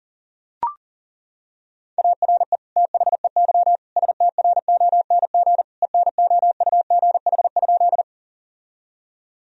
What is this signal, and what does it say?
Morse code, heard as a keyed tone, about 40 wpm, 700 Hz. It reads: ARE THEY STRONG ENOUGH?
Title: Sentences from Top 1000 Words - Review Part 3 - 40wpm